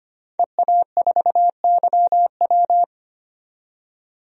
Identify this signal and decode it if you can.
Morse code, heard as a keyed tone, about 25 wpm, 700 Hz. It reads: EA4YW
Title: DXCC-100 Call Signs - 5 Characters - 25wpm